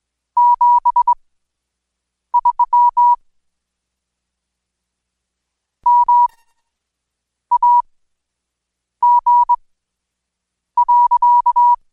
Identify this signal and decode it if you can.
Morse code, heard as a keyed tone, about 21 wpm, spaced as for 6 wpm, 970 Hz. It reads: 73 MAG.
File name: Code Quick/AudioDownld2/Seg109-Day14-DayDrill 14